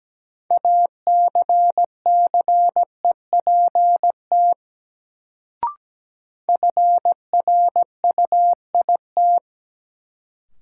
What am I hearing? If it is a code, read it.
Morse code, heard as a keyed tone, about 17 wpm, 700 Hz. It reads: ACCEPT FRUIT
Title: Sentences from Top 1300 Words - New Word Drill Alt - 17wpm